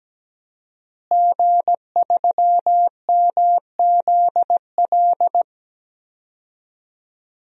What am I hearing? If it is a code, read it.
Morse code, heard as a keyed tone, about 17 wpm, 700 Hz. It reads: G3MZL
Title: DXCC-100 Call Signs - 5 Characters - Repeated 3x - 17wpm